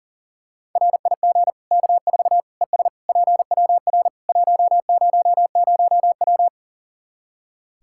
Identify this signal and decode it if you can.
Morse code, heard as a keyed tone, about 40 wpm, 700 Hz. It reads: RIG K4 ES PWR 100W
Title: Practice QSO - Exchange 2 - Info - 40wpm